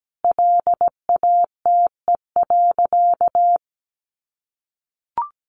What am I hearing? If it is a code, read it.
Morse code, heard as a keyed tone, about 17 wpm, 700 Hz. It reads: LATE.
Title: Sentences from Top 300 Words 17wpm